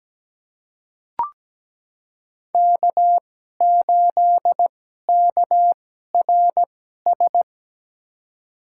Morse code, heard as a keyed tone, about 17 wpm, spaced as for 12 wpm, 700 Hz. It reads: K8KRS
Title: USA Call Signs - 5 Characters - 2x Character Spacing - 17wpm